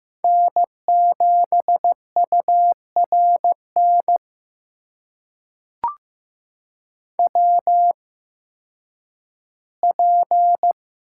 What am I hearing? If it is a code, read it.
Morse code, heard as a keyed tone, about 15 wpm, spaced as for 10 wpm, 700 Hz. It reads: N7URN WP
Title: USA Call Signs - 5 Characters - 8x Character Spacing - 15wpm